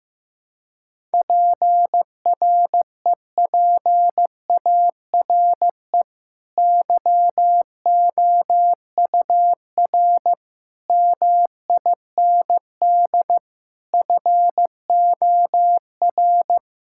Morse code, heard as a keyed tone, about 15 wpm, 700 Hz. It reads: PREPARE YOUR MIND FOR
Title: Sentences from Top 1000 Words - Review Part 6 - 15wpm